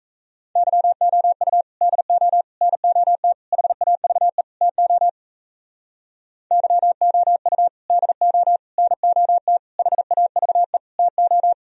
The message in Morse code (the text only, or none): YOU DO NOT HAVE TO YOU DO NOT HAVE TO